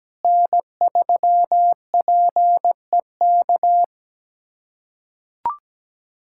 Morse code, heard as a keyed tone, about 17 wpm, 700 Hz. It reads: N3PEK
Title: USA Callsign DE Callsign 17wpm